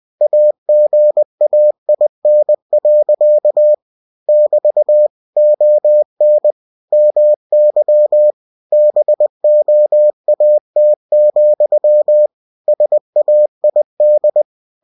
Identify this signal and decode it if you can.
Morse code, heard as a keyed tone, about 20 wpm, 600 Hz. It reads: AGAIN. <BT> ON MY BOAT, SAID
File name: cw-books / winnie-the-pooh / winnie-20wpm-page-0098